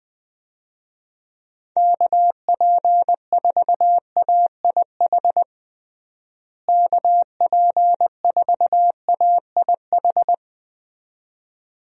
Morse code, heard as a keyed tone, about 21 wpm, 700 Hz. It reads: KP4AIH KP4AIH
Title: DXCC-100 Call Signs - 6 Characters - Repeated 2x - 20wpm